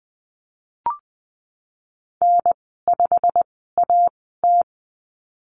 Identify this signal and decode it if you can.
Morse code, heard as a keyed tone, about 20 wpm, spaced as for 15 wpm, 700 Hz. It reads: N5AT
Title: 140 Lesson 23g - Letter f - Call Signs 20wpm